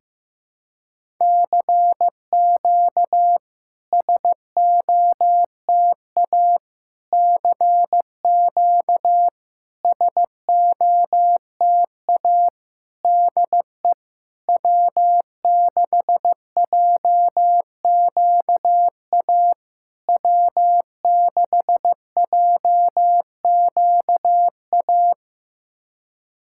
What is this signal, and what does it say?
Morse code, heard as a keyed tone, about 15 wpm, 700 Hz. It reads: CQ SOTA CQ SOTA DE W6JQA W6JQA
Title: SOTA QSO Practice 15wpm